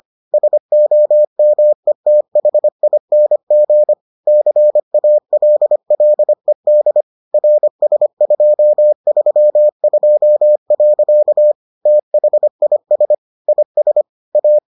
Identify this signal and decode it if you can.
Morse code, heard as a keyed tone, about 25 wpm, 600 Hz. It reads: SOMETHING CALLED RS232. THIS IS A